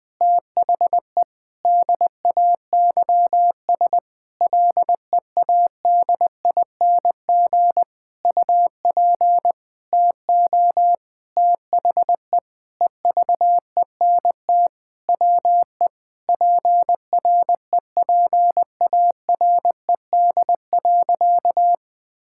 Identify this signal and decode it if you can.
Morse code, heard as a keyed tone, about 20 wpm, 700 Hz. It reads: THE DAYS LEADING UP TO THE EVENT WE PREPARED.
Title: Sentences with 6-Word Phrases - Spoken Beginning - 20wpm